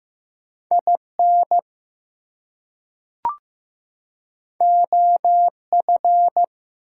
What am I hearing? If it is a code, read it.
Morse code, heard as a keyed tone, about 15 wpm, 700 Hz. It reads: IN OF
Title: Two Letter Words - 15wpm